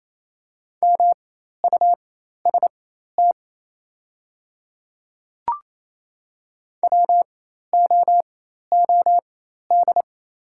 Morse code, heard as a keyed tone, about 28 wpm, spaced as for 13 wpm, 700 Hz. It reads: MUST WOOD
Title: 4x Character Spacing - Top 300 Words - 28wpm